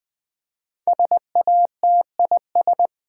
Morse code, heard as a keyed tone, about 20 wpm, 700 Hz. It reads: SATIS
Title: Build A Word - 5 Letters At A Time - 12-Letter Words - 20wpm